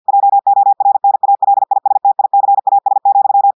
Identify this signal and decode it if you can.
Morse code, heard as a keyed tone, about 50 wpm, 800 Hz. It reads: JORNALISTICAS,
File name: portugais texte en clair 2 vitesse 50 mots minutes